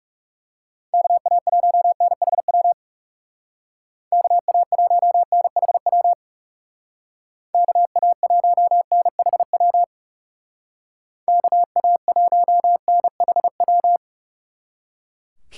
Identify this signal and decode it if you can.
Morse code, heard as a keyed tone, about 36 wpm, 700 Hz. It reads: KA1NHW KA1NHW KA1NHW KA1NHW
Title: Speed Racing - USA 2x3 Call Signs 30wpm